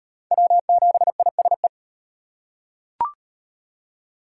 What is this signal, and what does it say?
Morse code, heard as a keyed tone, about 35 wpm, 700 Hz. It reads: W7ISE